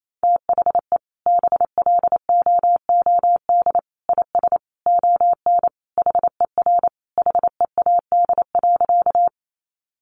Morse code, heard as a keyed tone, about 28 wpm, 700 Hz. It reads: THE BLOOD IS ON HER HEAD.
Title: Sentences from Top 1100 Words 28wpm